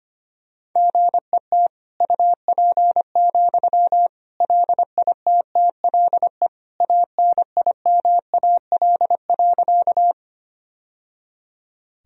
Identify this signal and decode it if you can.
Morse code, heard as a keyed tone, about 25 wpm, 700 Hz. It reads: GET UP, LITTLE ANIMAL.
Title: Sentences from Top 1000 Words - Review Part 1 - 25wpm